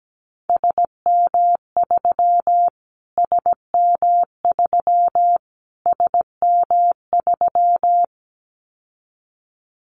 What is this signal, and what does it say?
Morse code, heard as a keyed tone, about 17 wpm, 700 Hz. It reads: SM3 SM3 SM3
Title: Callsign Prefixes for Top 25 DXCC - Repeated 3x - 17wpm